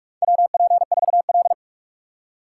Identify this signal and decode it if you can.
Morse code, heard as a keyed tone, about 44 wpm, 700 Hz. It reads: WP4L
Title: USA Call Signs - 4 Characters - 2x Character Spacing - 45wpm